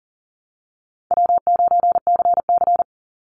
Morse code, heard as a keyed tone, about 40 wpm, 700 Hz. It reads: W9CC
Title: Easy Call Signs - 40wpm